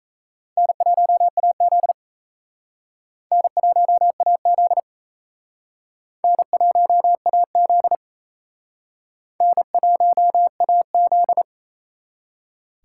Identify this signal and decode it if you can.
Morse code, heard as a keyed tone, about 35 wpm, 700 Hz. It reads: N1AZ N1AZ N1AZ N1AZ
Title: Speed Racing - USA 1x2 Call Signs 28wpm